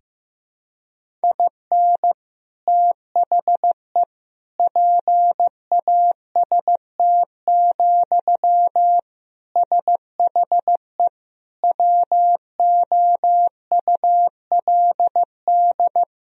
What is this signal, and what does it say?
Morse code, heard as a keyed tone, about 15 wpm, 700 Hz. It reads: IN THE PAST, SHE WOULD
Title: Sentences from Top 1000 Words - Review Part 3 - 15wpm